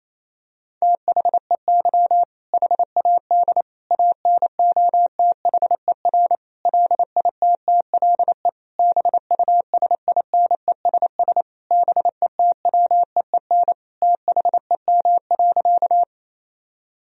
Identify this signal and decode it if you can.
Morse code, heard as a keyed tone, about 28 wpm, 700 Hz. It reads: THEY HAD ANOTHER LITTLE BUSINESS BETWEEN THEM.
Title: Sentences from Top 1000 Words - Review Part 1 - 28wpm